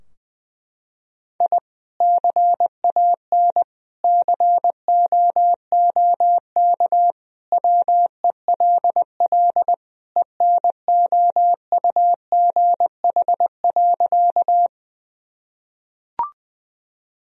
Morse code, heard as a keyed tone, about 20 wpm, 700 Hz. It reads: I CAN COOK WELL ENOUGH.
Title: Sentences from Top 1000 Words - Review Part 4 - 20wpm